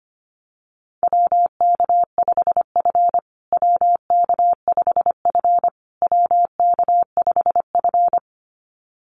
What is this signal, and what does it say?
Morse code, heard as a keyed tone, about 25 wpm, 700 Hz. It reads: WK5F WK5F WK5F
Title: USA Call Signs Repeated 3x 25wpm